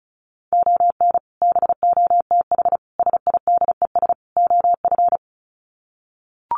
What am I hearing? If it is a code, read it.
Morse code, heard as a keyed tone, about 34 wpm, 700 Hz. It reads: ON BOTH SIDES OF
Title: Sets of 4 Words 35wpm